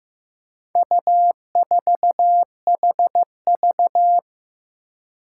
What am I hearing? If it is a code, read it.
Morse code, heard as a keyed tone, about 15 wpm, 700 Hz. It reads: U4HV
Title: Four Challenging Letter-Number Combinations 15wpm